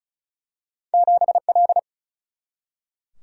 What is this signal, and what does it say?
Morse code, heard as a keyed tone, about 36 wpm, 700 Hz. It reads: 7L
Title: Single-Letter Single-Number 35wpm